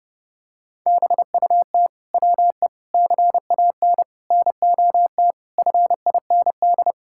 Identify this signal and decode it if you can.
Morse code, heard as a keyed tone, about 32 wpm, 700 Hz. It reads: BUT WE CAN NOT FIND
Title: Sentences from Top 1000 Words - Review Part 5 - 30wpm